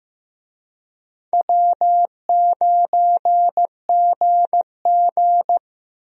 Morse code, heard as a keyed tone, about 15 wpm, 700 Hz. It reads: W9GG
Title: USA Call Signs - 4 Characters - Repeated 3x - 15wpm